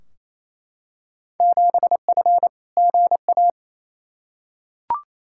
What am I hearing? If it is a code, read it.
Morse code, heard as a keyed tone, about 28 wpm, 700 Hz. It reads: 7F GA
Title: Field Day Exchanges Encore - 28wpm